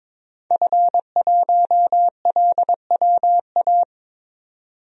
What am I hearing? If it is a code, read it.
Morse code, heard as a keyed tone, about 22 wpm, 700 Hz. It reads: F1LWA
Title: DXCC-100 Call Signs - 5 Characters - 22wpm